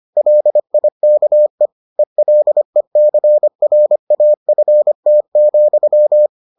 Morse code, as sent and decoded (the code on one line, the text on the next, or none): .-.. .. -.- . / . .-.. -.-. .-. .- ..-. - --..--
LIKE ELECRAFT,